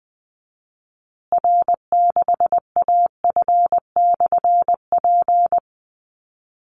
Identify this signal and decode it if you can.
Morse code, heard as a keyed tone, about 20 wpm, 700 Hz. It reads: R6AF/P
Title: Call Signs Repeated 2x - 20wpm